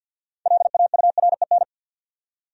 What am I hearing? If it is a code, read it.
Morse code, heard as a keyed tone, about 51 wpm, 700 Hz. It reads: LAUREN
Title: USA Common Names - 50wpm